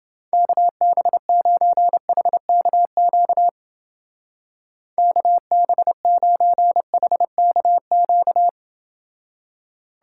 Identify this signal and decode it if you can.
Morse code, heard as a keyed tone, about 28 wpm, 700 Hz. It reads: KB9HKQ KB9HKQ